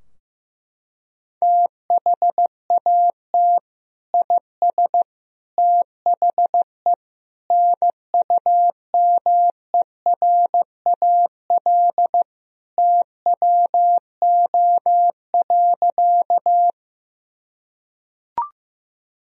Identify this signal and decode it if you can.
Morse code, heard as a keyed tone, about 16 wpm, 700 Hz. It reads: THAT IS THE NUMERAL TWO.
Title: Sentences from Top 1000 Words - Review Part 2 - 15wpm